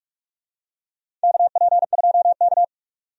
Morse code, heard as a keyed tone, about 41 wpm, 700 Hz. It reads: KP2K
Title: USA Call Signs - 4 Characters - 45wpm